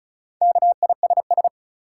Morse code, heard as a keyed tone, about 36 wpm, 700 Hz. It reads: KISS
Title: Four Letter Words - 35wpm